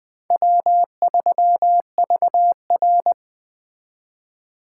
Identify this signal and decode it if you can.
Morse code, heard as a keyed tone, about 20 wpm, 700 Hz. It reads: W3VR
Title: Speed Racing - USA 1x2 Call Signs 17wpm